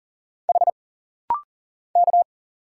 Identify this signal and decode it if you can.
Morse code, heard as a keyed tone, about 40 wpm, 700 Hz. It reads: H K
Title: Single Letters - Rapid-Fire 40wpm